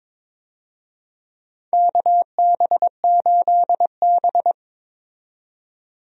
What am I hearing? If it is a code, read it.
Morse code, heard as a keyed tone, about 22 wpm, 700 Hz. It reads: KB8B
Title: USA Call Signs - 4 Characters - 22wpm